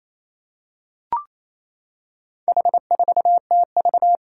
Morse code, heard as a keyed tone, about 28 wpm, 700 Hz. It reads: H4TV
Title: Four Challenging Letter-Number Combinations 28wpm